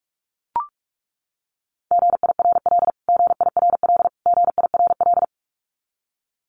Morse code, heard as a keyed tone, about 44 wpm, 700 Hz. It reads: GIRL GIRL GIRL
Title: Top 500 Words Repeated 45wpm